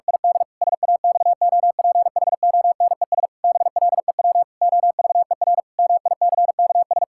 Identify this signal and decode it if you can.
Morse code, heard as a keyed tone, about 48 wpm, 700 Hz. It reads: ID SAXOPHONES BLEW OVER MICKS
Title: Pangram Challenge v2 - 45wpm